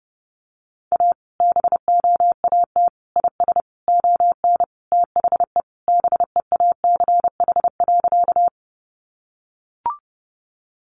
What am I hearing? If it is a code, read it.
Morse code, heard as a keyed tone, about 30 wpm, 700 Hz. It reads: A BOAT IS ON THE BEACH.
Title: Sentences from Top 1500 Words 30wpm